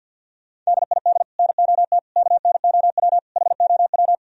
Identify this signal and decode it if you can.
Morse code, heard as a keyed tone, about 64 wpm, 690 Hz. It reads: DID NOT KNOW HOW